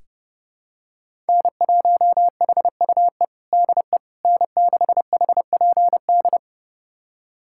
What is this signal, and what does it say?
Morse code, heard as a keyed tone, about 32 wpm, 700 Hz. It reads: N1HUE DE N6HPD
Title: USA Callsign DE Callsign 30wpm